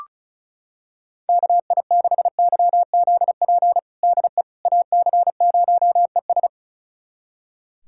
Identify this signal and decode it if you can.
Morse code, heard as a keyed tone, about 34 wpm, 700 Hz. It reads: KI6YZP DE AC0ES
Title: USA Callsign DE Callsign 35wpm